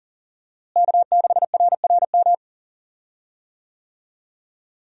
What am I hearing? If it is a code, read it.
Morse code, heard as a keyed tone, about 40 wpm, 700 Hz. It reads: K6RRM
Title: USA Call Signs - 5 Characters - Repeated 2x - 40wpm